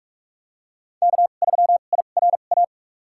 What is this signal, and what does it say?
Morse code, heard as a keyed tone, about 47 wpm, spaced as for 33 wpm, 700 Hz. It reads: K3IRA
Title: USA Call Signs - 5 Characters - 2x Character Spacing - 45wpm